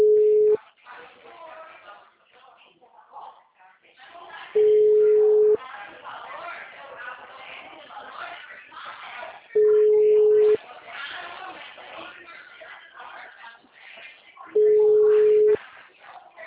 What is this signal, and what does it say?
Ringback tone (Europe, 425 Hz), with speech.
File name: out- -2011- - - .12956